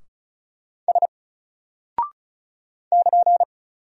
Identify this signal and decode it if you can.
Morse code, heard as a keyed tone, about 35 wpm, 700 Hz. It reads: S <KN>
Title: Single Character (Letter:Number:Punctuation:Prosign) 35wpm